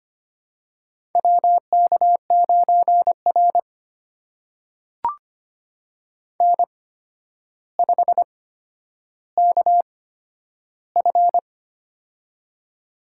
Morse code, heard as a keyed tone, about 25 wpm, spaced as for 11 wpm, 700 Hz. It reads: WK9R N5KF
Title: USA Call Signs - 4 Characters - 8x Character Spacing - 25wpm